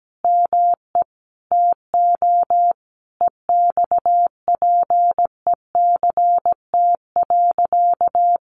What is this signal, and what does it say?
Morse code, heard as a keyed tone, about 17 wpm, 700 Hz. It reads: ME TO EXPECT.
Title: Sentences from Top 1200 Words 17wpm